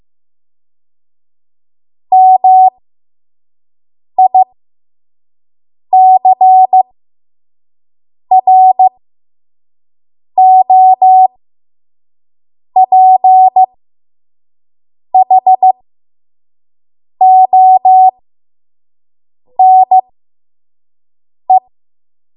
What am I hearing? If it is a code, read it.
Morse code, heard as a keyed tone, about 15 wpm, spaced as for 5 wpm, 750 Hz. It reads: MICROPHONE
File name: Morse / 05WPM